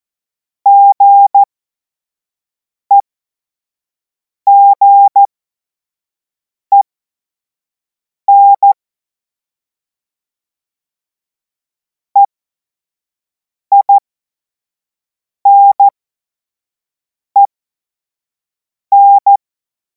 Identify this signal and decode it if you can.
Morse code, heard as a keyed tone, about 14 wpm, spaced as for 5 wpm, 800 Hz. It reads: GEGEN EINEN
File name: Morsekurs / Witze / Witz 24. März 2026 25 F70